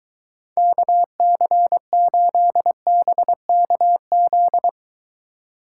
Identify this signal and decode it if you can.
Morse code, heard as a keyed tone, about 23 wpm, 700 Hz. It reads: KC8BKZ